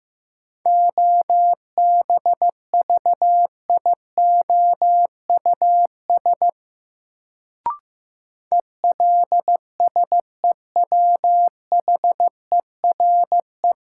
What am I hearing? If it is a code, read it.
Morse code, heard as a keyed tone, about 15 wpm, 700 Hz. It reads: OBVIOUS ELSEWHERE
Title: Sentences from Top 1600 Words - New Word Drill - 15wpm